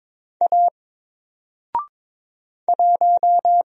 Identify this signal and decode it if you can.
Morse code, heard as a keyed tone, about 22 wpm, 700 Hz. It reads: A 1
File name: Single-Letter-Number-22wpm